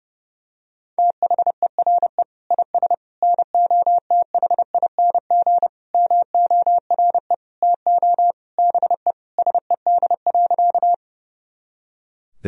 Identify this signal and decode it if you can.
Morse code, heard as a keyed tone, about 30 wpm, 700 Hz. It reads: THERE IS NOTHING MORE TO BE SED.
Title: Sentences from Top 500 Words and Abbreviations 30wpm